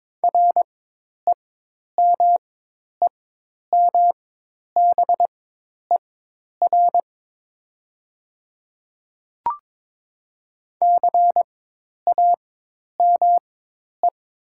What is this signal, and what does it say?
Morse code, heard as a keyed tone, about 22 wpm, spaced as for 10 wpm, 700 Hz. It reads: REMEMBER CAME
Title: 4x Character Spacing - Top 500 Words - 22wpm